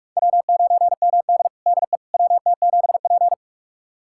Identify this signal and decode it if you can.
Morse code, heard as a keyed tone, about 44 wpm, 700 Hz. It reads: W9MD DE WT7P
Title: USA Callsign DE Callsign - 4 Characters - 45wpm